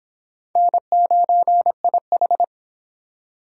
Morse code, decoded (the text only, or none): N9IH